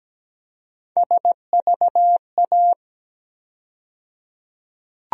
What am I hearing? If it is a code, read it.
Morse code, heard as a keyed tone, about 17 wpm, 700 Hz. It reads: SVA
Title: USA Call Sign Postfixes - 3 Letters 17wpm